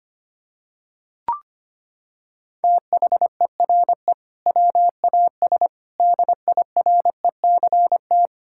Morse code, heard as a keyed tone, about 25 wpm, 700 Hz. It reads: THERE WAS DIRECT
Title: Sentences from Top 1000 Words - Review Part 6 - 25wpm